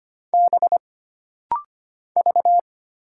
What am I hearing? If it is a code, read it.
Morse code, heard as a keyed tone, about 25 wpm, 700 Hz. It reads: B V
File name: Single Character (Letter:Number:Punctuation:Prosign) - Rapid-Fire - 25wpm